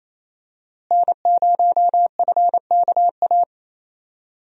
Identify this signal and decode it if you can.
Morse code, heard as a keyed tone, about 28 wpm, 700 Hz. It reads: N0FKA
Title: USA Call Signs - 5 Characters - Repeated 3x - 28wpm